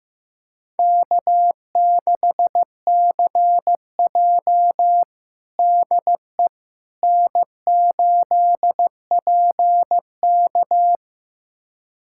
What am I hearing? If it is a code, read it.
Morse code, heard as a keyed tone, about 15 wpm, 700 Hz. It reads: K6CJ DE N8PK
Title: USA Callsign DE Callsign - 4 Characters - 15wpm